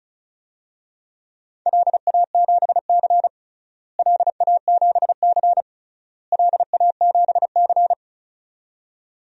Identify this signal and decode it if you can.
Morse code, heard as a keyed tone, about 34 wpm, 700 Hz. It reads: LA7C LA7C LA7C
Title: DXCC-100 Call Signs - 4 Characters - Repeated 3x - 35wpm